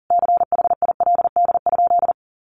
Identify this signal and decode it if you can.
Morse code, heard as a keyed tone, about 40 wpm, 700 Hz. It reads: CHILD?
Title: Sentences from Top 1000 Words 40wpm